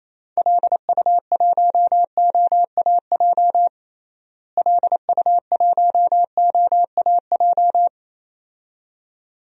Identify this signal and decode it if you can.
Morse code, heard as a keyed tone, about 27 wpm, 700 Hz. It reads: LU1OAJ LU1OAJ
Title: DXCC-100 Call Signs - 6 Characters - Repeated 2x - 28wpm